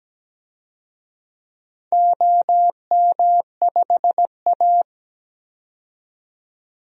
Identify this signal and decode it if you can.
Morse code, heard as a keyed tone, about 17 wpm, 700 Hz. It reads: OM5A
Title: DXCC-100 Call Signs - 4 Characters - 17wpm